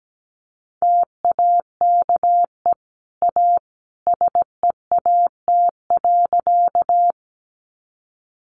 Morse code, heard as a keyed tone, about 17 wpm, 700 Hz. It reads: TAKE A SEAT.
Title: Sentences from Top 1500 Words 17wpm